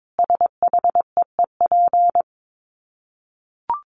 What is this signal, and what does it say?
Morse code, heard as a keyed tone, about 22 wpm, 700 Hz. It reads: SHEEP
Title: Dit Challenge 22wpm